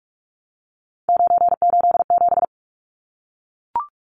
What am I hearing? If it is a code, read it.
Morse code, heard as a keyed tone, about 44 wpm, 700 Hz. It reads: 987
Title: Numbers - 3-Digit 45wpm